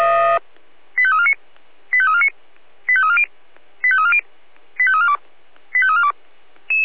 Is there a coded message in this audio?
Yes: five-tone selective call (ZVEI) from 1.0 to 6.1 s, heard as quick runs of five tones, tones of about 70 ms. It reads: 85349 85349 85340 85340 85342 85342